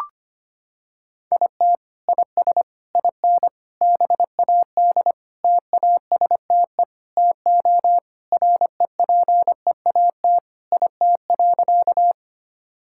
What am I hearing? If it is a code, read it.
Morse code, heard as a keyed tone, about 25 wpm, 700 Hz. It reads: IT IS IN BAD TASTE TO REPEAT IT.
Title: Sentences from Top 1600 Words 25wpm